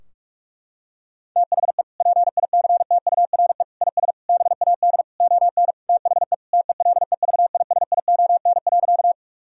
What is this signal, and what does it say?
Morse code, heard as a keyed tone, about 47 wpm, 700 Hz. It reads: THE PICTURE IS BAD ON THE TELEVISION.
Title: Sentences from Top 1000 Words - Review Part 4 - 45wpm